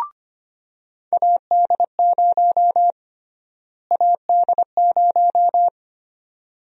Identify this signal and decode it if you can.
Morse code, heard as a keyed tone, about 25 wpm, 700 Hz. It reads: AD0 AD0
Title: USA Call Sign Prefixes - Repeated 2x - 25wpm